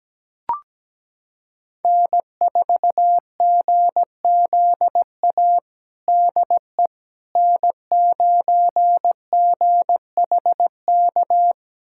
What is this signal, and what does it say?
Morse code, heard as a keyed tone, about 17 wpm, 700 Hz. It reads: N4GZA DE N9GHK
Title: USA Callsign DE Callsign - 5 Characters - 17wpm